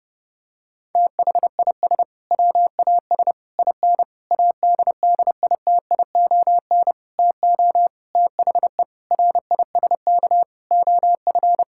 Morse code, heard as a keyed tone, about 30 wpm, 700 Hz. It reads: THIS WAS IN ADDITION TO THE RISK OF
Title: Sentences from Top 1000 Words - Review Part 6 - 30wpm